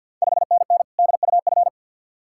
Morse code, heard as a keyed tone, about 46 wpm, 700 Hz. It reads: 5NN DUF
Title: Ontario QSO Party Exchange 50wpm